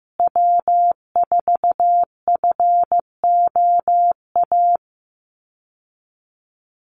Morse code, heard as a keyed tone, about 15 wpm, 700 Hz. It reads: W4FOA
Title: Call Signs Repeated 3x - 15wpm